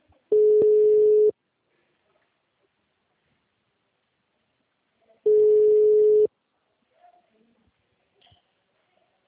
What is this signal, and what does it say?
Ringback tone (Europe, 425 Hz).